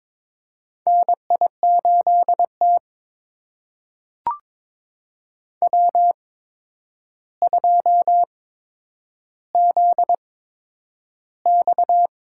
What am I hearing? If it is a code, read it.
Morse code, heard as a keyed tone, about 22 wpm, spaced as for 9 wpm, 700 Hz. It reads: NI8T W2ZX
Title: USA Call Signs - 4 Characters - 8x Character Spacing - 22wpm